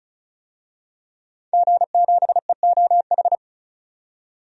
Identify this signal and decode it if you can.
Morse code, heard as a keyed tone, about 36 wpm, 700 Hz. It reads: G7EOH
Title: DXCC-100 Call Signs - 5 Characters - 35wpm